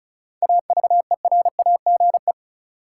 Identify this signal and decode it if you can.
Morse code, heard as a keyed tone, about 36 wpm, 700 Hz. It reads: AVERAGE
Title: Sets of 5 Words - New Word Drill - 35wpm